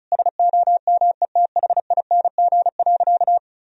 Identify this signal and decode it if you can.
Morse code, heard as a keyed tone, about 36 wpm, 700 Hz. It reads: SOMETHING.
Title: Sentences from Top 1000 Words - Review Part 1 - 35wpm